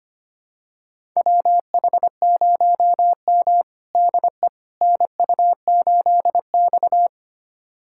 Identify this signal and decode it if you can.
Morse code, heard as a keyed tone, about 25 wpm, 700 Hz. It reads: WH0M DE NU8X
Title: USA Callsign DE Callsign - 4 Characters - 25wpm